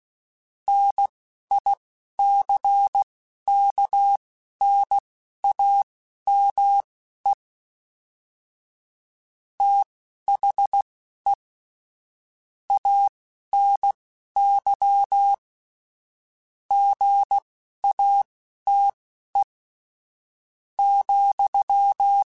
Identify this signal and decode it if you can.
Morse code, heard as a keyed tone, about 16 wpm, spaced as for 11 wpm, 800 Hz. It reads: NICKNAME THE ANY GATE,
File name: Technical Subjects / Technical Subjects-mp3s / OR Gate @10